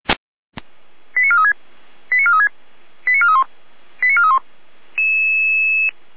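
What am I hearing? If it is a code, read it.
Five-tone selective call (ZVEI), heard as quick runs of five tones, tones of about 70 ms. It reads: 89436 89436 89431 89431
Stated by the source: ZVEI five-tone call received by SDR.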